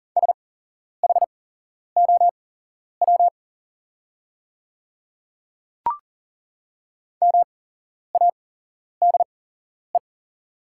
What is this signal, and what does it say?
Morse code, heard as a keyed tone, about 40 wpm, spaced as for 11 wpm, 690 Hz. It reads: SHOW MADE
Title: 8x Character Spacing - Top 300 Words - 40wpm